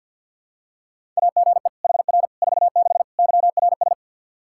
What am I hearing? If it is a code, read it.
Morse code, heard as a keyed tone, about 52 wpm, 700 Hz. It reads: AGE HR 46 YRS
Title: Practice QSO - Exchange 3 - More Info - 50wpm